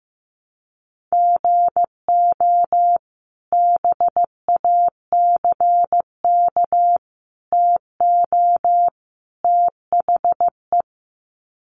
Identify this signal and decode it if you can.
Morse code, heard as a keyed tone, about 14 wpm, 700 Hz. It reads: GO BACK TO THE
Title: Sets of 4 Words 15wpm